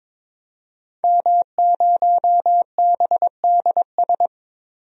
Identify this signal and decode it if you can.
Morse code, heard as a keyed tone, about 22 wpm, 700 Hz. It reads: M0BDS
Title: DXCC-100 Call Signs - 5 Characters - 22wpm